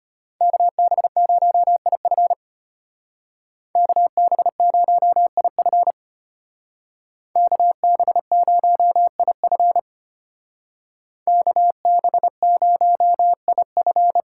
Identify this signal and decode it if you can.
Morse code, heard as a keyed tone, about 31 wpm, 700 Hz. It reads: KB0IF KB0IF KB0IF KB0IF
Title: Speed Racing - USA 2x2 Call Signs 25wpm